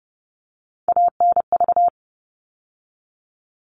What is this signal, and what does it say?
Morse code, heard as a keyed tone, about 30 wpm, 700 Hz. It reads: ANV
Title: All Cut Numbers - 3-Digit 30wpm